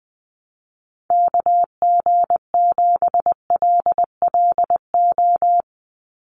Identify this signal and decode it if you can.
Morse code, heard as a keyed tone, about 20 wpm, 700 Hz. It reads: KG7LLO
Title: USA Call Signs 20wpm